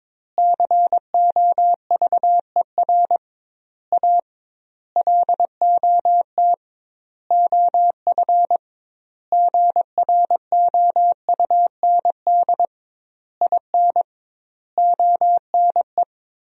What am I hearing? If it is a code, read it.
Morse code, heard as a keyed tone, about 22 wpm, 700 Hz. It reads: COVER A LOT OF GROUND IN ONE
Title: 2x Word-Spacing - Sentences from Top 500 Words - 22wpm